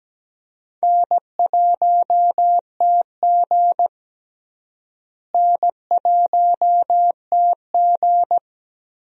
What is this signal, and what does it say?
Morse code, heard as a keyed tone, about 17 wpm, 700 Hz. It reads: N1TG N1TG
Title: USA Call Signs - 4 Characters - Repeated 3x - 17wpm